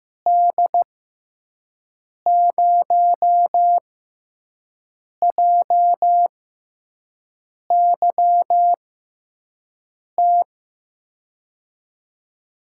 Morse code, heard as a keyed tone, about 15 wpm, spaced as for 5 wpm, 700 Hz. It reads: D0JYT
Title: USA Call Signs - 6 Characters - 6x Character Spacing - 15wpm